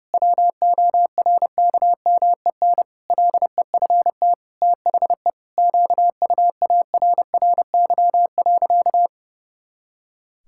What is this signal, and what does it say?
Morse code, heard as a keyed tone, about 30 wpm, 700 Hz. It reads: WORKMEN LEFT THE QUARRY.
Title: Pangram Challenge v2 - 30wpm